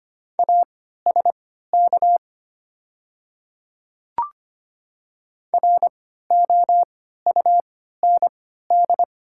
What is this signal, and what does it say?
Morse code, heard as a keyed tone, about 25 wpm, spaced as for 14 wpm, 690 Hz. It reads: ASK ROUND
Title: 3x Character Spacing - Top 200 Words - 25wpm